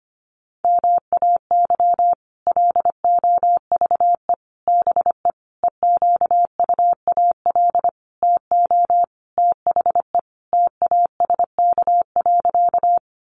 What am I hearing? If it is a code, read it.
Morse code, heard as a keyed tone, about 25 wpm, 700 Hz. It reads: MAY LOVE BE EQUAL TO THE TASK.
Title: Sentences from Top 1400 Words 25wpm